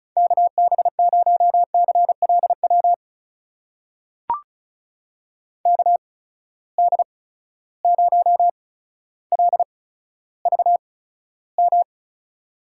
Morse code, heard as a keyed tone, about 36 wpm, spaced as for 15 wpm, 700 Hz. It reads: KB0CLW KD0LVM
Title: USA Call Signs - 6 Characters - 8x Character Spacing - 35wpm